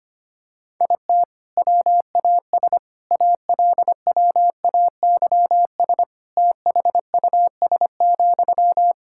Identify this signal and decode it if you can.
Morse code, heard as a keyed tone, about 25 wpm, 700 Hz. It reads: IT WAS ALWAYS THUS,
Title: Sentences from Top 1000 Words - Review Part 5 - 25wpm